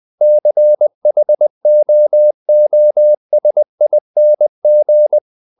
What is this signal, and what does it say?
Morse code, heard as a keyed tone, about 20 wpm, 600 Hz. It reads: CHOOSING
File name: CW / Practice / EnglishQuotations0320